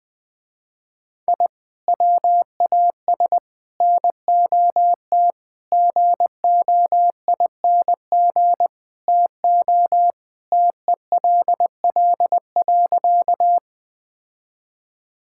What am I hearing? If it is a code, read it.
Morse code, heard as a keyed tone, about 20 wpm, 700 Hz. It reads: I WAS NOT GOING TO TELL.
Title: Sentences with 5-Word Phrases - Spoken Beginning - 20wpm